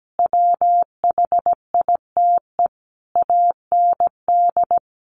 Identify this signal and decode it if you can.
Morse code, heard as a keyed tone, about 17 wpm, 700 Hz. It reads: WHITE AND
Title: Sentences from Top 1500 Words 17wpm